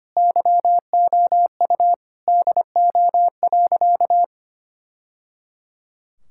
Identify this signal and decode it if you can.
Morse code, heard as a keyed tone, about 25 wpm, 700 Hz. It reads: YOU DO.
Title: Sentences from Top 1000 Words - Review Part 6 - 25wpm